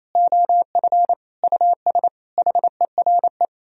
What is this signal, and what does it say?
Morse code, heard as a keyed tone, about 36 wpm, 700 Hz. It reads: OF US HERE